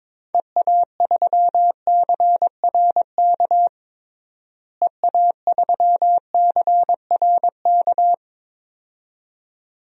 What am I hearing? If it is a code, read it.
Morse code, heard as a keyed tone, about 21 wpm, 700 Hz. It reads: EA3CRK EA3CRK
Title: DXCC-100 Call Signs - 6 Characters - Repeated 2x - 22wpm